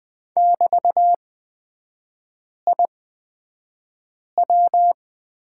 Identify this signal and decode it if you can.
Morse code, heard as a keyed tone, about 20 wpm, spaced as for 5 wpm, 700 Hz. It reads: <BT>IW
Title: 224 Lesson 36e - BT - Three Characters 20wpm